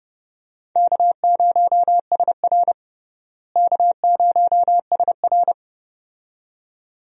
Morse code, heard as a keyed tone, about 31 wpm, 700 Hz. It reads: K0SR K0SR
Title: USA Call Signs - 4 Characters - Repeated 2x - 30wpm